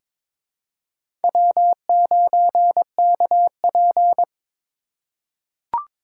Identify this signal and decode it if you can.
Morse code, heard as a keyed tone, about 21 wpm, 700 Hz. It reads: W9KP
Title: USA Call Signs - 3x Character Spacing - 22wpm